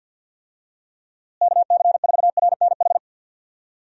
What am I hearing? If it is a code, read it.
Morse code, heard as a keyed tone, about 46 wpm, 700 Hz. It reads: KC4RNH
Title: USA Call Signs - 6 Characters - Repeated 3x - 50wpm